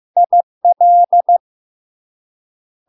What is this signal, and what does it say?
Morse code, heard as a keyed tone, about 15 wpm, 700 Hz. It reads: IL
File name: LARK-FD-2022-Log-in-CW-15wpm-700-Hz